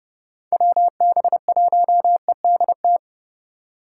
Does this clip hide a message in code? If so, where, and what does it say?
Yes: Morse code from 0.5 to 3.0 s, heard as a keyed tone, about 38 wpm, 700 Hz. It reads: WB1EDT